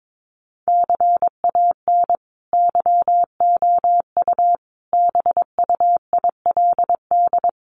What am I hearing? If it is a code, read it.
Morse code, heard as a keyed tone, about 22 wpm, 700 Hz. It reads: CAN YOU BUILD
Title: Sentences from Top 1400 Words 22wpm